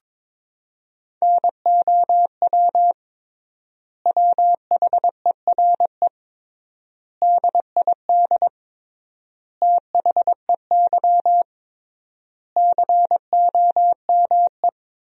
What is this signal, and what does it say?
Morse code, heard as a keyed tone, about 22 wpm, 700 Hz. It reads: NOW WHERE DID THEY COME
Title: 3x Word-Spacing - Sentences from Top 200 Words - 22wpm